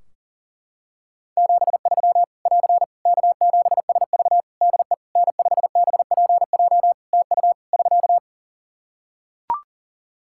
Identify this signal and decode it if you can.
Morse code, heard as a keyed tone, about 40 wpm, 700 Hz. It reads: 73 <AR> K7SV DE N5BPJ TU <SK>
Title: Practice QSO - Exchange 4 - Ending - 40wpm